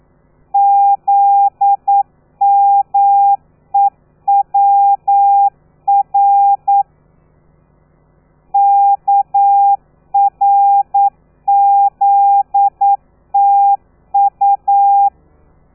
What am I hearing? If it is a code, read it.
Morse code, heard as a keyed tone, about 9 wpm, 800 Hz. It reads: ZMEWR KRZTU
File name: telegrafia / i2tfj / Lez13 40